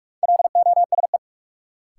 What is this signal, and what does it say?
Morse code, heard as a keyed tone, about 46 wpm, 700 Hz. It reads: LOSE
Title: Sentences from Top 1100 Words - New Word Drill Alt - 45wpm